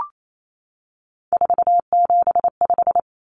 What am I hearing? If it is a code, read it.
Morse code, heard as a keyed tone, about 28 wpm, 700 Hz. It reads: 475
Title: Common Cut Numbers - 3-Digit 28wpm